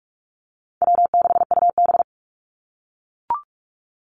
Morse code, heard as a keyed tone, about 44 wpm, 700 Hz. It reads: W6UB
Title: NA SOTA Activator Callsigns 45wpm